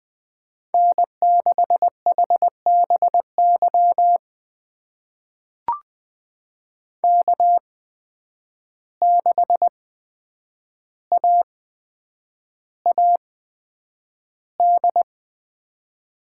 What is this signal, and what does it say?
Morse code, heard as a keyed tone, about 20 wpm, spaced as for 9 wpm, 700 Hz. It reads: N6HBY K6AAD
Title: USA Call Signs - 5 Characters - 8x Character Spacing - 20wpm